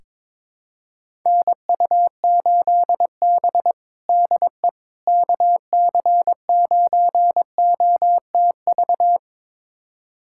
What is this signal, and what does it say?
Morse code, heard as a keyed tone, about 22 wpm, 700 Hz. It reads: NU8B DE KC9OTV
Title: USA Callsign DE Callsign 22wpm